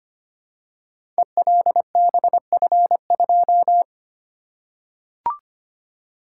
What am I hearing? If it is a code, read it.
Morse code, heard as a keyed tone, about 25 wpm, 700 Hz. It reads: ELBF2
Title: Code Groups - 5 Letter:Numbers - 25wpm